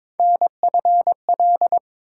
Morse code, heard as a keyed tone, about 22 wpm, 700 Hz. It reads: NFL
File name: ARRL Sections v2 - 22wpm